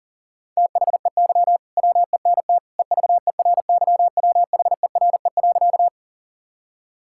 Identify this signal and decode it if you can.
Morse code, heard as a keyed tone, about 40 wpm, 700 Hz. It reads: THEY WENT EVERYWHERE.
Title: Sentences from Top 1600 Words 40wpm